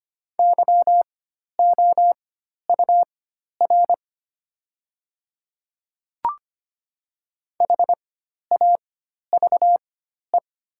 Morse code, heard as a keyed tone, about 25 wpm, spaced as for 12 wpm, 690 Hz. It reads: YOUR HAVE
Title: 4x Character Spacing - Top 100 Words - 25wpm